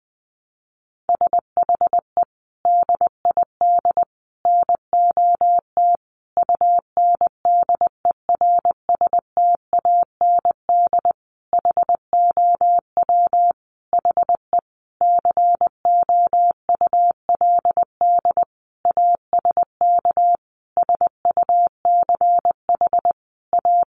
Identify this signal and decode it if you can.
Morse code, heard as a keyed tone, about 20 wpm, 700 Hz. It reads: SHE DID NOT UNDERSTAND HOW HE COULD ASK SUCH A
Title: Sentences from Top 600 Words 20wpm